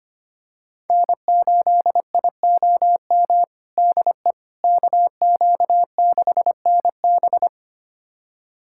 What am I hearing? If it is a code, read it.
Morse code, heard as a keyed tone, about 25 wpm, 700 Hz. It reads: N8IOM DE KQ6NB
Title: USA Callsign DE Callsign - 5 Characters - 25wpm